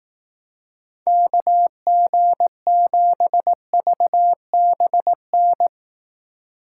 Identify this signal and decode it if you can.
Morse code, heard as a keyed tone, about 18 wpm, 700 Hz. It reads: KG7VBN
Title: Speed Racing - USA 2x3 Call Signs 15wpm